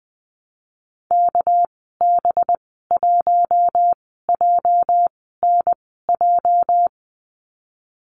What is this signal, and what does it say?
Morse code, heard as a keyed tone, about 20 wpm, spaced as for 15 wpm, 700 Hz. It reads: KB1JNJ
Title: 207 Lesson 33g - Letter j - Call Signs 20wpm